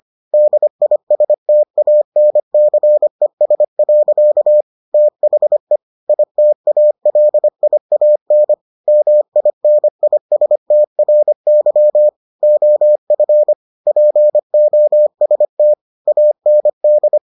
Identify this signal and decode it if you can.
Morse code, heard as a keyed tone, about 25 wpm, 600 Hz. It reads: DISTANCES. THE ITALIAN MINISTRY OF POST AND